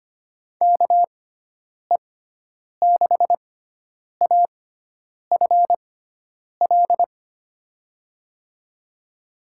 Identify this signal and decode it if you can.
Morse code, heard as a keyed tone, about 25 wpm, spaced as for 9 wpm, 700 Hz. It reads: KE6AFL
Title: USA Call Signs - 6x Character Spacing - 25wpm